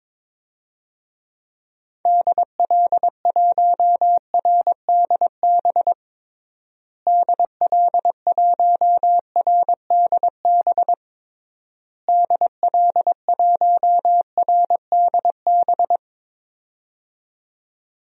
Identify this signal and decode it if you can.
Morse code, heard as a keyed tone, about 22 wpm, 700 Hz. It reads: DL1RDB DL1RDB DL1RDB
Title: DXCC-100 Call Signs - 6 Characters - Repeated 3x - 22wpm